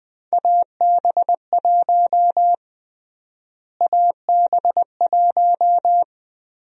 Morse code, heard as a keyed tone, about 20 wpm, 700 Hz. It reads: AB1 AB1
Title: USA Call Sign Prefixes - Repeated 2x - 20wpm